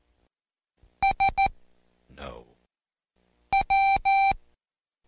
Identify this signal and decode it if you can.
Morse code, heard as a keyed tone, about 14 wpm, 780 Hz. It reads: S W
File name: Code Quick / AudioDownld1 / Seg04-Day1-Intro W